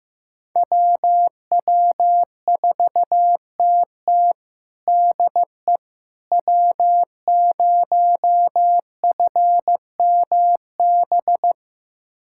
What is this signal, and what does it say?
Morse code, heard as a keyed tone, about 16 wpm, 700 Hz. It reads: WW4TT DE W0FMB
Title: USA Callsign DE Callsign - 5 Characters - 15wpm